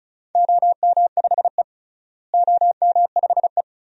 Morse code, heard as a keyed tone, about 36 wpm, 700 Hz. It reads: OM5E OM5E
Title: DXCC-100 Call Signs - 4 Characters - Repeated 2x - 35wpm